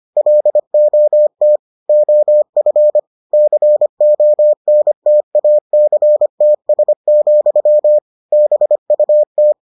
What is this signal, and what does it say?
Morse code, heard as a keyed tone, about 25 wpm, 600 Hz. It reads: LOT OF CONTACTS, BUT